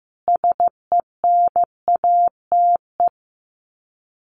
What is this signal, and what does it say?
Morse code, heard as a keyed tone, about 15 wpm, 700 Hz. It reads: SENATE
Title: Sets of 5 Words Encore2 15wpm